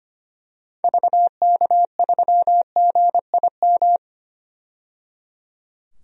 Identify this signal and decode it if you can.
Morse code, heard as a keyed tone, about 25 wpm, 700 Hz. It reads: VK3GIM
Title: DXCC-100 Call Signs - 6 Characters - 25wpm